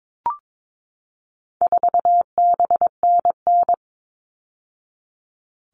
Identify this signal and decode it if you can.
Morse code, heard as a keyed tone, about 22 wpm, 700 Hz. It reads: 4BNN
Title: All Cut Numbers - 4-Digit 22wpm